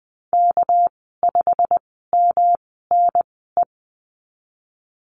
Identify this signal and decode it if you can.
Morse code, heard as a keyed tone, about 20 wpm, spaced as for 14 wpm, 700 Hz. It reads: K5MNE
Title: 238 Lesson 38g - Letter q - Call Signs 20wpm